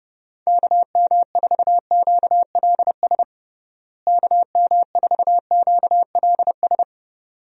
Morse code, heard as a keyed tone, about 32 wpm, 700 Hz. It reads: KM4QLS KM4QLS
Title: USA Call Signs - 6 Characters - Repeated 2x - 30wpm